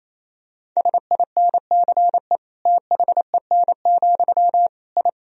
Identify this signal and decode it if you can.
Morse code, heard as a keyed tone, about 28 wpm, 700 Hz. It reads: SINCE THEN, I
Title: Sentences from Top 1000 Words - Review Part 3 - 28wpm